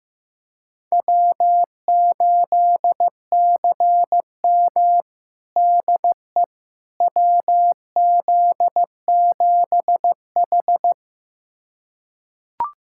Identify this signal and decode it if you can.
Morse code, heard as a keyed tone, about 15 wpm, 700 Hz. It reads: W8CM DE WZ7H
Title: USA Callsign DE Callsign - 4 Characters - 15wpm